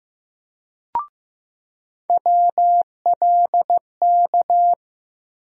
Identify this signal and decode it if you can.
Morse code, heard as a keyed tone, about 15 wpm, 700 Hz. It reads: WLK
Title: USA Call Sign Postfixes - 3 Letters 15wpm